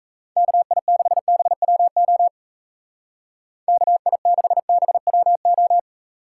Morse code, heard as a keyed tone, about 35 wpm, 700 Hz. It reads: KI6BWO KI6BWO
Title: Speed Racing - USA 2x3 Call Signs 28wpm